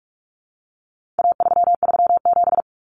Morse code, heard as a keyed tone, about 44 wpm, 700 Hz. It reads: A337
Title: Common Cut Numbers - 4-Digit 45wpm